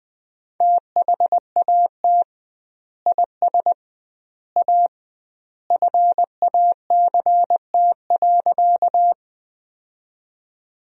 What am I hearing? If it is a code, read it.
Morse code, heard as a keyed tone, about 20 wpm, 700 Hz. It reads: THAT IS A FACT.
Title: 2x Word-Spacing - Sentences from Top 500 Words - 20wpm